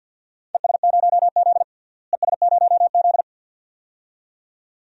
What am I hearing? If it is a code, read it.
Morse code, heard as a keyed tone, about 47 wpm, 700 Hz. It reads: ES0Z ES0Z
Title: DXCC-100 Call Signs - 4 Characters - Repeated 2x - 50wpm